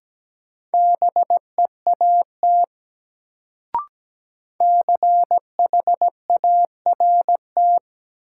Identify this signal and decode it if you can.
Morse code, heard as a keyed tone, about 17 wpm, 700 Hz. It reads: BEAT CHART
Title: Sentences from Top 1300 Words - New Word Drill Alt - 17wpm